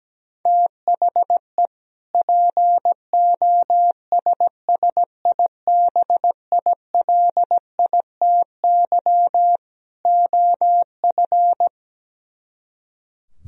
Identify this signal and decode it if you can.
Morse code, heard as a keyed tone, about 17 wpm, 700 Hz. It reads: THE POSSIBILITY OF
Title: Speed-Racing - Sets of 3 Words 17wpm